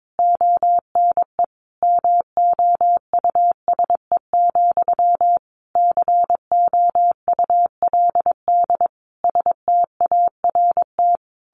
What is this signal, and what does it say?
Morse code, heard as a keyed tone, about 22 wpm, 700 Hz. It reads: ONE MOUSE, COULD START
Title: A Sound Of Thunder 22wpm